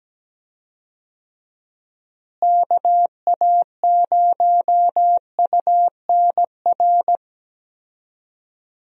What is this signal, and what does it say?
Morse code, heard as a keyed tone, about 17 wpm, 700 Hz. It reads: KA0UNR
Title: USA Call Signs - 6 Characters - Repeated 2x - 17wpm